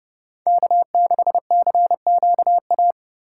Morse code, heard as a keyed tone, about 36 wpm, 700 Hz. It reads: K6CQA
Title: Speed Racing - USA 1x3 Call Signs 30wpm